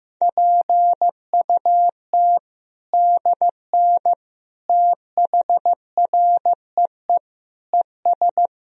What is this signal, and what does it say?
Morse code, heard as a keyed tone, about 15 wpm, 700 Hz. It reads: PUT DN THREE ES
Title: Sentences from Top 400 Words and Abbreviations 15wpm